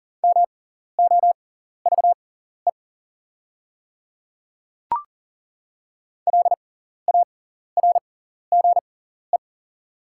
Morse code, heard as a keyed tone, about 37 wpm, spaced as for 14 wpm, 700 Hz. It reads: MOVE LARGE
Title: 6x Character Spacing - Top 300 Words - 40wpm